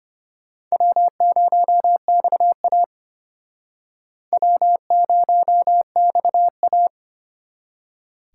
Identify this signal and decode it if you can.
Morse code, heard as a keyed tone, about 32 wpm, 700 Hz. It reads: W0XA W0XA